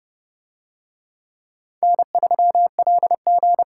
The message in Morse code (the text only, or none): N3LG